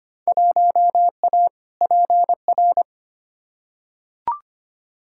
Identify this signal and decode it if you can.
Morse code, heard as a keyed tone, about 25 wpm, 700 Hz. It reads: 1A PR
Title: Field Day Exchanges Encore - 25wpm